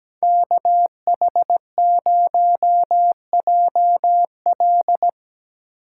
Morse code, heard as a keyed tone, about 17 wpm, 700 Hz. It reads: KH0JL
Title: USA Call Signs - 5 Characters - 6x Character Spacing - 17wpm